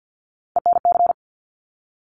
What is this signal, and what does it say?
Morse code, heard as a keyed tone, about 50 wpm, 700 Hz. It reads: ENC
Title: Top Three Letter Combinations in Words 50wpm